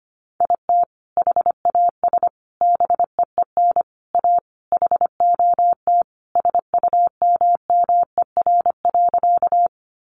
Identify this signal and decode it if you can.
Morse code, heard as a keyed tone, about 25 wpm, 700 Hz. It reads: IT HAS BEEN A HOT SUMMER.
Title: Sentences from Top 1100 Words 25wpm